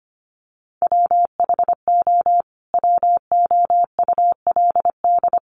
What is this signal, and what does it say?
Morse code, heard as a keyed tone, about 25 wpm, 700 Hz. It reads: WHO WOULD
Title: Sentences from Top 800 Words 25wpm